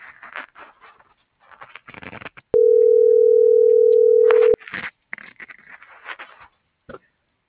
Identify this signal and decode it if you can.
Ringback tone (North America), 440 + 480 Hz.